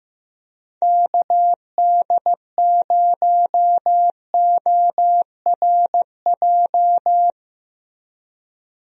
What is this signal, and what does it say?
Morse code, heard as a keyed tone, about 15 wpm, 700 Hz. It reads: KD0ORJ
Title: USA Call Signs - 6 Characters - 15wpm